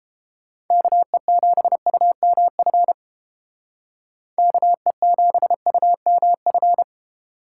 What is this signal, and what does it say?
Morse code, heard as a keyed tone, about 28 wpm, 700 Hz. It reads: KE7UMF KE7UMF